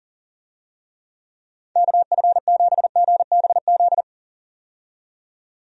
Morse code, heard as a keyed tone, about 40 wpm, 700 Hz. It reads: KF7GBZ
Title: USA Call Signs - 6 Characters - 40wpm